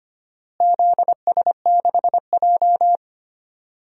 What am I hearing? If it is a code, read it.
Morse code, heard as a keyed tone, about 25 wpm, 700 Hz. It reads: ZS6J
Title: DXCC-100 Call Signs - 4 Characters - Repeated 3x - 25wpm